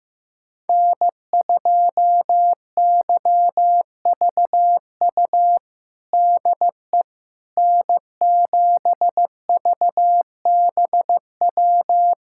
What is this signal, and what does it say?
Morse code, heard as a keyed tone, about 15 wpm, 700 Hz. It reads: N2YVU DE N7VBW
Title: USA Callsign DE Callsign 15wpm